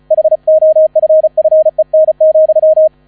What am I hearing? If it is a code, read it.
Morse code, heard as a keyed tone, about 36 wpm, 630 Hz. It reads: HOFFEN,
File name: 35wpm